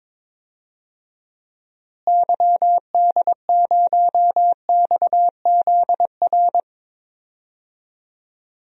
Morse code, heard as a keyed tone, about 22 wpm, 700 Hz. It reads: YD0XZR
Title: DXCC-100 Call Signs - 6 Characters - Repeated 2x - 22wpm